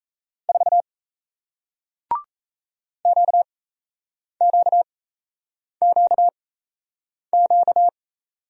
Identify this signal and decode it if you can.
Morse code, heard as a keyed tone, about 38 wpm, spaced as for 8 wpm, 700 Hz. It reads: 4 QQQQ